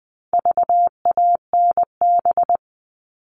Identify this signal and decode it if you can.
Morse code, heard as a keyed tone, about 20 wpm, 700 Hz. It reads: VANB
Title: All Cut Numbers - 4-Digit 20wpm